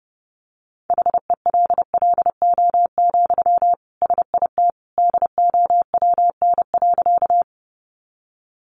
Morse code, heard as a keyed tone, about 30 wpm, 700 Hz. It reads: HELLO, SIT DOWN.
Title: Sentences from Top 1100 Words 30wpm